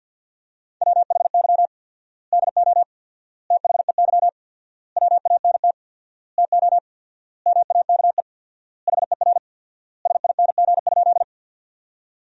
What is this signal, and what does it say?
Morse code, heard as a keyed tone, about 50 wpm, 700 Hz. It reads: WHY DO THEY WANT TO MAKE HER SING?
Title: 4x Word-Spacing - Sentences from Top 400 Words - 50wpm